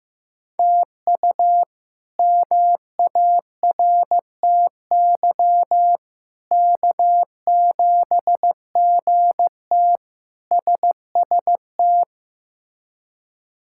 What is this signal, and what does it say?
Morse code, heard as a keyed tone, about 15 wpm, 700 Hz. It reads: TU MARTY K7GT SST
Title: K1USN SST QSO Practice - 15wpm